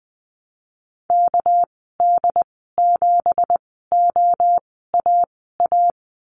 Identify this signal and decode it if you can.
Morse code, heard as a keyed tone, about 20 wpm, spaced as for 14 wpm, 700 Hz. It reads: KD7OAA
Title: 219 Lesson 35g - Number 0 - Call Signs 20wpm